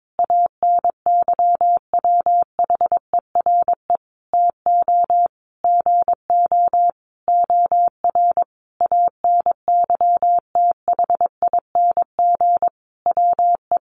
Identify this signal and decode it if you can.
Morse code, heard as a keyed tone, about 22 wpm, 700 Hz. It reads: ANYWHERE TO GO OR ANYTHING WE
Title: Sentences from Top 1400 Words 22wpm